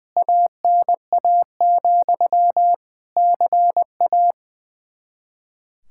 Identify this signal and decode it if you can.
Morse code, heard as a keyed tone, about 20 wpm, 700 Hz. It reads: ANA, CA
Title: Practice QSO - Exchange 1 - Beginning - 20wpm